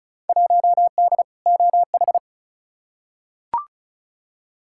Morse code, heard as a keyed tone, about 36 wpm, 700 Hz. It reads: 1D OH
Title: Field Day Exchanges - 35wpm